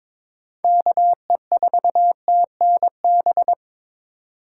Morse code, heard as a keyed tone, about 21 wpm, 700 Hz. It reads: KE4TNB
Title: USA Call Signs - 4x Character Spacing - 22wpm